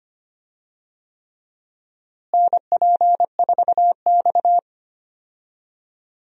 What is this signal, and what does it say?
Morse code, heard as a keyed tone, about 25 wpm, 700 Hz. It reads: NP4X
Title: USA Call Signs - 4 Characters - Repeated 3x - 25wpm